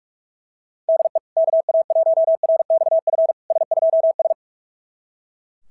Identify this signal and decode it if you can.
Morse code, heard as a keyed tone, about 41 wpm, 660 Hz. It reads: DE KA1RXF S2S
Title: SOTA QSO Practice 45wpm